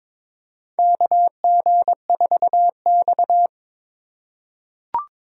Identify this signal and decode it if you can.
Morse code, heard as a keyed tone, about 22 wpm, 700 Hz. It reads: KG4X
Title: USA Call Signs - 4 Characters - 3x Character Spacing - 22wpm